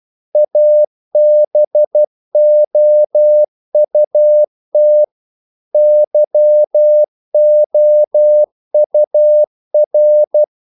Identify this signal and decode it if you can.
Morse code, heard as a keyed tone, about 12 wpm, 600 Hz. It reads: ABOUT YOUR
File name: CW / Practice / If12